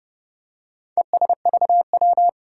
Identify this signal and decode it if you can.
Morse code, heard as a keyed tone, about 30 wpm, 700 Hz. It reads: ESVW
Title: Four Challenging Letter-Number Combinations 30wpm